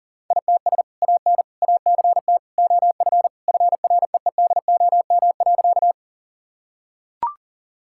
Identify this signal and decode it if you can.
Morse code, heard as a keyed tone, about 40 wpm, 700 Hz. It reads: ITS AN ACT OF FREEDOM.
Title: Sentences from Top 1000 Words - Review Part 5 - 40wpm